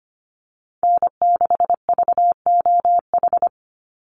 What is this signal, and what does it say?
Morse code, heard as a keyed tone, about 25 wpm, 700 Hz. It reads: N6VOH
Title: Call Signs Repeated 3x - 25wpm